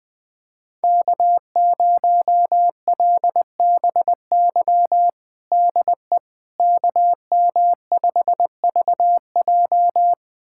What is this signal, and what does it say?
Morse code, heard as a keyed tone, about 21 wpm, 700 Hz. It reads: K0LBY DE KM5VJ
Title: USA Callsign DE Callsign - 5 Characters - 20wpm